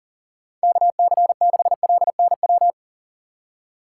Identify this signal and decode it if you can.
Morse code, heard as a keyed tone, about 40 wpm, 700 Hz. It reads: KC6LNW
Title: USA Call Signs - 6 Characters - 3x Character Spacing - 40wpm